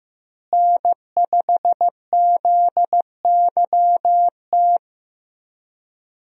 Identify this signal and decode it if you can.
Morse code, heard as a keyed tone, about 15 wpm, 700 Hz. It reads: N5ZYT
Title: USA Call Signs - 5 Characters - Repeated 2x - 15wpm